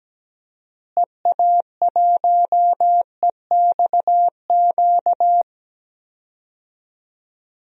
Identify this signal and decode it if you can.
Morse code, heard as a keyed tone, about 17 wpm, 700 Hz. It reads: EA1EXQ
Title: DXCC-100 Call Signs - 6 Characters - Repeated 2x - 17wpm